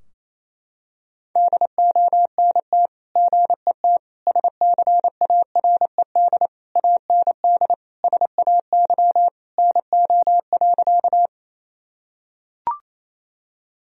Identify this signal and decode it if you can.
Morse code, heard as a keyed tone, about 28 wpm, 700 Hz. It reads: DONT GET SCARED AND SAY NO.
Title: Sentences from Top 1000 Words - Review Part 4 - 28wpm